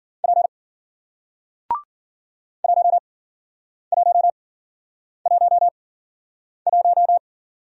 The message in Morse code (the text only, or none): P 1111